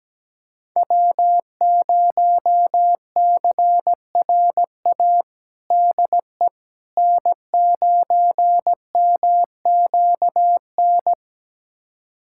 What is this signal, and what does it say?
Morse code, heard as a keyed tone, about 17 wpm, 700 Hz. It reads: W0CRA DE N9MQN
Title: USA Callsign DE Callsign - 5 Characters - 17wpm